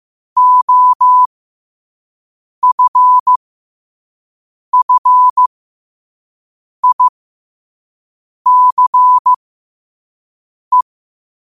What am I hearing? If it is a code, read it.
Morse code, heard as a keyed tone, about 15 wpm, spaced as for 5 wpm, 1000 Hz. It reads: OFFICE
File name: qotdfiles / 5wpm